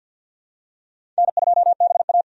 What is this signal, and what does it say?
Morse code, heard as a keyed tone, about 53 wpm, 700 Hz. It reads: N2BA
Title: USA Call Signs - 4 Characters - 50wpm